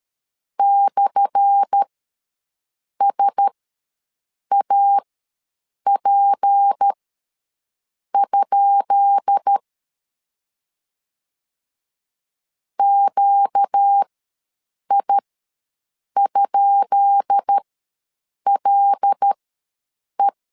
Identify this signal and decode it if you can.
Morse code, heard as a keyed tone, about 13 wpm, spaced as for 6 wpm, 800 Hz. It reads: /SAP? QI?LE
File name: LEZIONI / 10 SIGNS / 10B INT ws 12wpm 3527